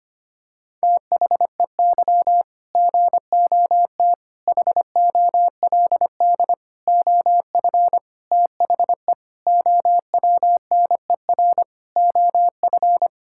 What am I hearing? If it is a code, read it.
Morse code, heard as a keyed tone, about 25 wpm, 700 Hz. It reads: THEY GOT HOLD OF THE OWNER OF
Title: Sentences from Top 1600 Words 25wpm